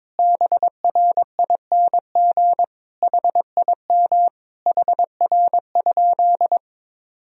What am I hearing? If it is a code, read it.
Morse code, heard as a keyed tone, about 22 wpm, 700 Hz. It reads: BRING HIM HR?
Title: Sentences from Top 500 Words and Abbreviations 22wpm